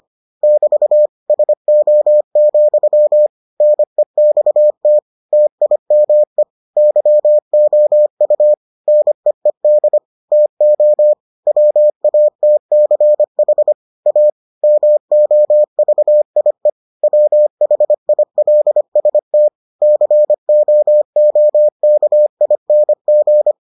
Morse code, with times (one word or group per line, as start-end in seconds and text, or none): 0.4-1.1 <BT>
1.3-3.3 SO,
3.6-5.0 NEXT
5.3-6.4 TIME
6.8-8.5 YOU
8.9-10.0 NEED
10.3-11.1 TO
11.5-13.7 WATCH
14.1-14.3 A
14.6-16.7 MOVIE
17.0-19.5 WHILST
19.8-23.5 COOKING